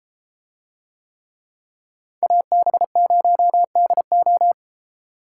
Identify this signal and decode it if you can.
Morse code, heard as a keyed tone, about 32 wpm, 700 Hz. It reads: AB0DO